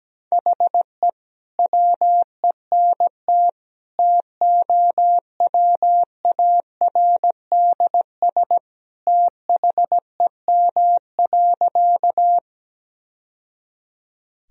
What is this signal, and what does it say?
Morse code, heard as a keyed tone, about 17 wpm, 700 Hz. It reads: HE WENT TOWARDS THEM.
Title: Sentences from Top 1000 Words - Review Part 2 - 17wpm